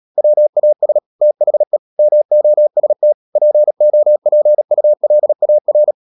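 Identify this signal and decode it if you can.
Morse code, heard as a keyed tone, about 37 wpm, 600 Hz. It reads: WAS THE MOST POPULAR
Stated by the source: Morse code practice at 35–39 wpm sending English text of Ken Follett's novel Never, from CW med Gnister.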